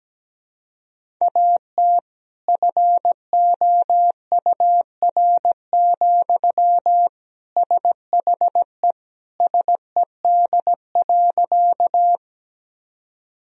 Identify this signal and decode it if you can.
Morse code, heard as a keyed tone, about 17 wpm, 700 Hz. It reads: AT FOUR, SHE SED.
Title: Sentences from Top 300 Words and Abbreviations 17wpm